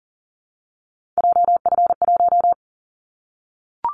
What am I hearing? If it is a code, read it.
Morse code, heard as a keyed tone, about 40 wpm, 700 Hz. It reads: JF1
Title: Callsign Prefixes for Top 25 DXCC - Repeated 3x - 40wpm